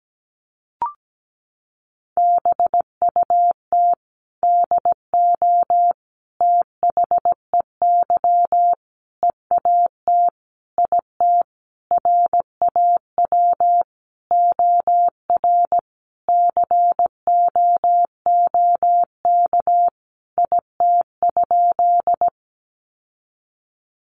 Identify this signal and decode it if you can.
Morse code, heard as a keyed tone, about 17 wpm, 700 Hz. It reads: BUT DO THEY EAT IT RAW OR COOK IT?
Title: Sentences from Top 1500 Words 17wpm